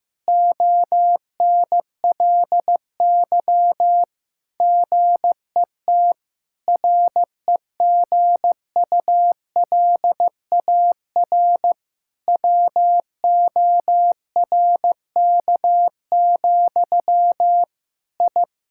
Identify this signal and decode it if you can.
Morse code, heard as a keyed tone, about 16 wpm, 700 Hz. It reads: ONLY GET REGULAR WORK, I
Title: Sentences from Top 1000 Words - Review Part 6 - 15wpm